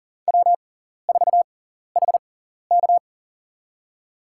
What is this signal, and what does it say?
Morse code, heard as a keyed tone, about 40 wpm, spaced as for 14 wpm, 700 Hz. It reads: W4HK
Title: USA Call Signs - 6x Character Spacing - 40wpm